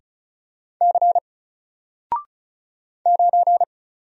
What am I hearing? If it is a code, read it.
Morse code, heard as a keyed tone, about 36 wpm, 700 Hz. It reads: C 9
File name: Single-Letter-Number-35wpm